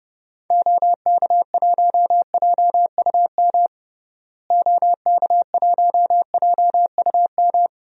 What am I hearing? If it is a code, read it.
Morse code, heard as a keyed tone, about 31 wpm, 700 Hz. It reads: OK1JUM OK1JUM
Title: DXCC-100 Call Signs - 6 Characters - Repeated 2x - 30wpm